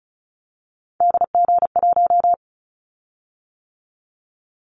Morse code, heard as a keyed tone, about 36 wpm, 700 Hz. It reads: DG1
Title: Callsign Prefixes for Top 25 DXCC - 35wpm